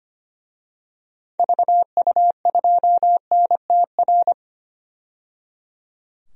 Morse code, heard as a keyed tone, about 25 wpm, 700 Hz. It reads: VU2NTR
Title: DXCC-100 Call Signs - 6 Characters - 25wpm